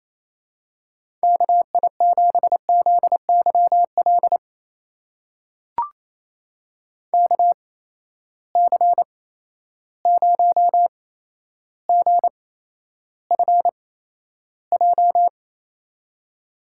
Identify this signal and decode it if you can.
Morse code, heard as a keyed tone, about 28 wpm, spaced as for 12 wpm, 700 Hz. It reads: KI7ZYL KC0GFJ
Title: USA Call Signs - 6 Characters - 8x Character Spacing - 28wpm